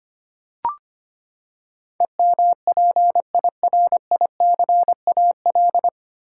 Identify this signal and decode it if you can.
Morse code, heard as a keyed tone, about 25 wpm, 700 Hz. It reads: EMPIRICAL
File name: Nine Letter Words - 25wpm